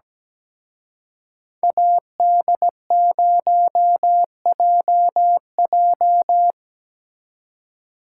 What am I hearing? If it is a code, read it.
Morse code, heard as a keyed tone, about 19 wpm, 700 Hz. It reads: AD0JJ
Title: Speed Racing - USA Call Signs 17wpm